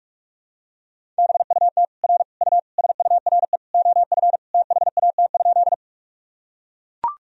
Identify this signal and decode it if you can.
Morse code, heard as a keyed tone, about 42 wpm, 700 Hz. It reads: BUT R U SURE OF THAT?
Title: Sentences from Top 300 Words and Abbreviations 45wpm